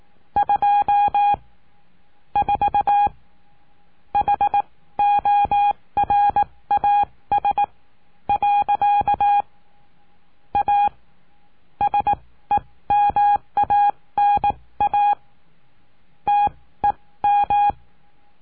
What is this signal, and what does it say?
Morse code, heard as a keyed tone, about 18 wpm, spaced as for 13 wpm, 820 Hz. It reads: 24 HORAS. A SEMANA TEM
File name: cursos / CURSO DE TELEGRAFIA / Telegrafia - Trilha 20